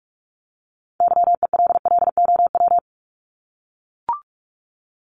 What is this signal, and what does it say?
Morse code, heard as a keyed tone, about 44 wpm, 700 Hz. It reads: YELLOW
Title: Top 1751-2000 Words 45wpm